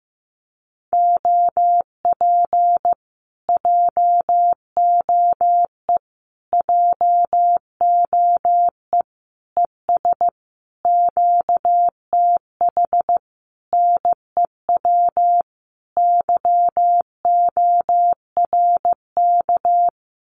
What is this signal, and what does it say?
Morse code, heard as a keyed tone, about 15 wpm, 700 Hz. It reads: OP JOE JOE ES QTH NEW YORK
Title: Common QSO Phrases Encore 15wpm